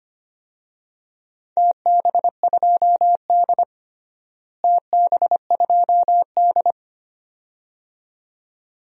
Morse code, heard as a keyed tone, about 25 wpm, 700 Hz. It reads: TB2D TB2D
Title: DXCC-100 Call Signs - 4 Characters - Repeated 2x - 25wpm